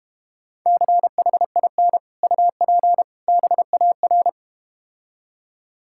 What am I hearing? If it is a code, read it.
Morse code, heard as a keyed tone, about 32 wpm, 700 Hz. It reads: CHIN UP BAR